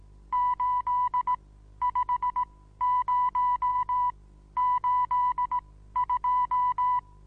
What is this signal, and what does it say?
Morse code, heard as a keyed tone, about 18 wpm, spaced as for 13 wpm, 990 Hz. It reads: 85082
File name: Onlinelehrgang / morsen / DJ4UF Morsen25